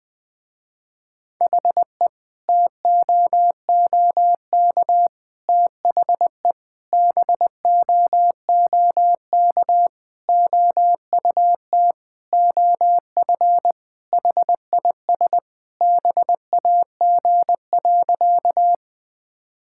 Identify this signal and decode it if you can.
Morse code, heard as a keyed tone, about 20 wpm, 700 Hz. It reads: HE TOOK THE BOOK OUT OF HIS BAG.
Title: Sentences with 3-Word Phrases - Spoken Beginning - 20wpm